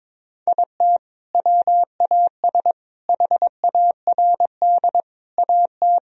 Morse code, heard as a keyed tone, about 22 wpm, 700 Hz. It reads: IT WAS HARD AT
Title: Sentences from Top 1000 Words - Review Part 1 - 22wpm